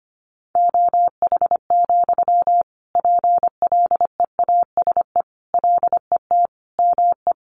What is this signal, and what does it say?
Morse code, heard as a keyed tone, about 25 wpm, 700 Hz. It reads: OH, PLEASE LET ME
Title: Sentences from Top 1500 Words 25wpm